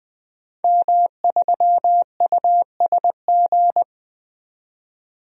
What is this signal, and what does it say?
Morse code, heard as a keyed tone, about 20 wpm, 700 Hz. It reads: M3USG
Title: DXCC-100 Call Signs - 5 Characters - 20wpm